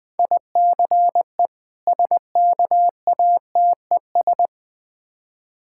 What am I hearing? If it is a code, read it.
Morse code, heard as a keyed tone, about 21 wpm, 700 Hz. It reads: ICE SKATES
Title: Difficult Words with Hints - 20wpm